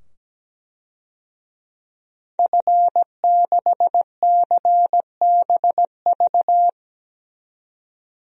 Morse code, heard as a keyed tone, about 17 wpm, 700 Hz. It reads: F6CBV
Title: DXCC-100 Call Signs - 5 Characters - 17wpm